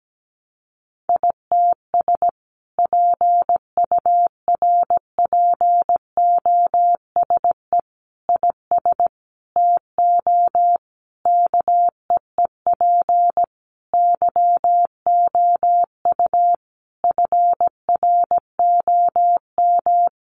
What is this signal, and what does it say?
Morse code, heard as a keyed tone, about 17 wpm, 700 Hz. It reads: ITS PURPOSE IS TO KEEP YOU FROM
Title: A Sound Of Thunder 17wpm